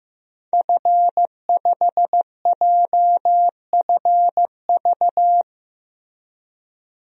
Morse code, heard as a keyed tone, about 15 wpm, 700 Hz. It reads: F5JFV
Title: DXCC-100 Call Signs - 5 Characters - Repeated 2x - 15wpm